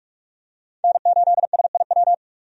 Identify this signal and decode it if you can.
Morse code, heard as a keyed tone, about 42 wpm, 700 Hz. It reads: N8SIW
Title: USA Call Signs - 5 Characters - 45wpm